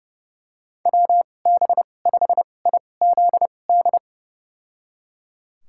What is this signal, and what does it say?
Morse code, heard as a keyed tone, about 32 wpm, spaced as for 22 wpm, 700 Hz. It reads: WB5IZD
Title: USA Call Signs - 2x Character Spacing - 30wpm